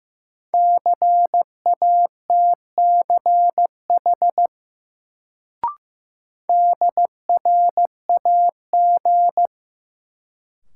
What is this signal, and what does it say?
Morse code, heard as a keyed tone, about 16 wpm, 700 Hz. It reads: CATCH DRAG
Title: Sentences from Top 1500 Words - New Word Drill - 15wpm